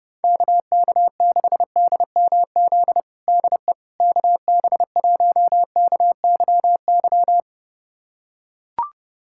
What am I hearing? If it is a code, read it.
Morse code, heard as a keyed tone, about 30 wpm, 700 Hz. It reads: KK6DMZ DE KB1KYY
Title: USA Callsign DE Callsign - 6 Characters - 30wpm